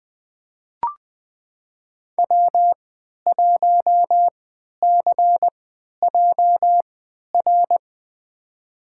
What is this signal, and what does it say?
Morse code, heard as a keyed tone, about 20 wpm, spaced as for 11 wpm, 700 Hz. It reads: W1CJR
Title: USA Call Signs - 5 Characters - 3x Character Spacing - 20wpm